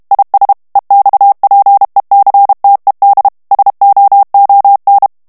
Morse code, heard as a keyed tone, about 32 wpm, 800 Hz. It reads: IS EXPECTED SOON
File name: morse mp3 / ANGLAIS texte clair 3 vitesse 30 mots minute